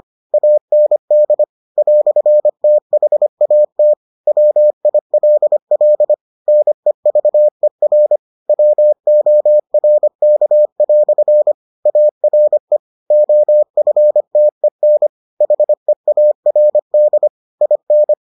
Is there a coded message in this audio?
Yes: Morse code from 0.3 to 18.1 s, heard as a keyed tone, about 25 wpm, 600 Hz. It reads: AND "THAT WILL NEVER WORK" ARE OFTEN HEARD IN